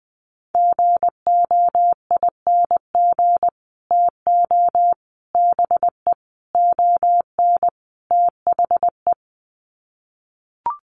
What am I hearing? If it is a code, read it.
Morse code, heard as a keyed tone, about 19 wpm, 700 Hz. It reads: GOING TO BE ON THE
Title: Sets of 5 Words Encore2 20wpm